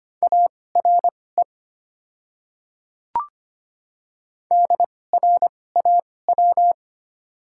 Morse code, heard as a keyed tone, about 25 wpm, spaced as for 18 wpm, 690 Hz. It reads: ARE DRAW
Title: 2x Character Spacing - Top 300 Words - 25wpm